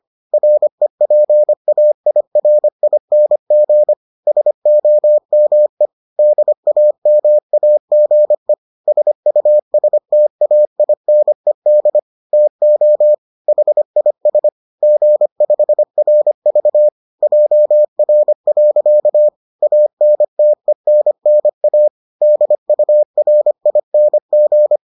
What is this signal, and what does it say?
Morse code, heard as a keyed tone, about 25 wpm, 600 Hz. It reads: REPAIRING SOME DAMAGE SUSTAINED TO HIS G5RV JR. ANTENNA DURING